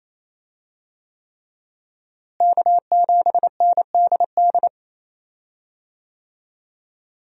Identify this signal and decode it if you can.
Morse code, heard as a keyed tone, about 28 wpm, 700 Hz. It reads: K7NDD
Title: USA Call Signs - 5 Characters - Repeated 3x - 28wpm